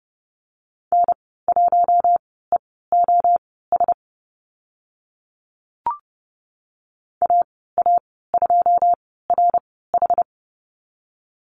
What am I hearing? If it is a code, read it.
Morse code, heard as a keyed tone, about 30 wpm, spaced as for 17 wpm, 700 Hz. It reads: N1EOS AA2RH
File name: Lesson 15g - Number 5 - Call Signs 30wpm